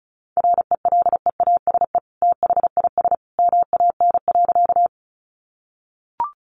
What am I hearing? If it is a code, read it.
Morse code, heard as a keyed tone, about 34 wpm, 700 Hz. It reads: RELEASE THIS MAN.
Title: Sentences from Top 1100 Words 35wpm